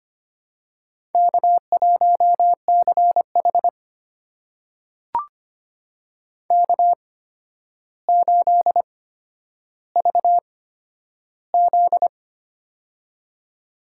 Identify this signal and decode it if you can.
Morse code, heard as a keyed tone, about 25 wpm, spaced as for 11 wpm, 700 Hz. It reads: K1CH K8VZ
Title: USA Call Signs - 4 Characters - 8x Character Spacing - 25wpm